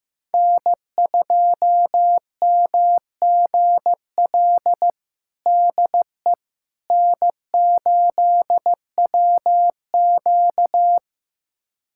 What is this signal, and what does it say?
Morse code, heard as a keyed tone, about 16 wpm, 700 Hz. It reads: N2MGL DE N8WQ
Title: USA Callsign DE Callsign 15wpm